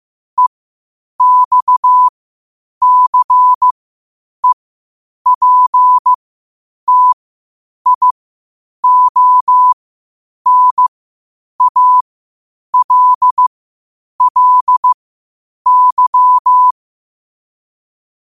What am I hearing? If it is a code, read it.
Morse code, heard as a keyed tone, about 15 wpm, spaced as for 8 wpm, 1000 Hz. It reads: EXCEPTIONALLY
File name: qotdfiles / 8wpm